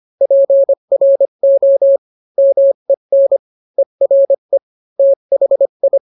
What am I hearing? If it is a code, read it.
Morse code, heard as a keyed tone, about 25 wpm, 550 Hz. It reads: PRO MEN ERE THI